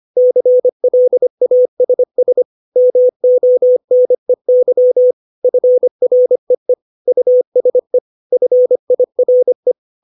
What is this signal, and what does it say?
Morse code, heard as a keyed tone, about 25 wpm, 500 Hz. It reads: CLASS MONEY FREE USE FIRE
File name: 632 words / 632 Words 25WPM